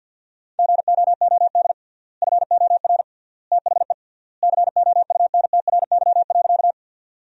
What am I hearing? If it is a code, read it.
Morse code, heard as a keyed tone, about 50 wpm, 700 Hz. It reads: GOOD FOR THE COUNTRY.
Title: 3x Word-Spacing - Sentences from Top 300 Words - 50wpm